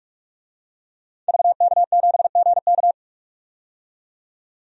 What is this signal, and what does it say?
Morse code, heard as a keyed tone, about 41 wpm, 700 Hz. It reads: VK7GK
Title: DXCC-100 Call Signs - 5 Characters - 45wpm